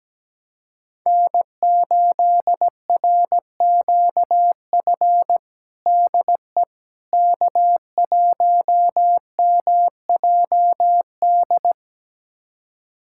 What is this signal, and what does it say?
Morse code, heard as a keyed tone, about 17 wpm, 700 Hz. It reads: N8RQF DE K1MJD
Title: USA Callsign DE Callsign - 5 Characters - 17wpm